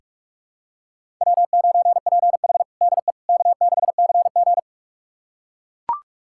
Morse code, heard as a keyed tone, about 41 wpm, 700 Hz. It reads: W9PH DE K6CG
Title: USA Callsign DE Callsign - 4 Characters - 45wpm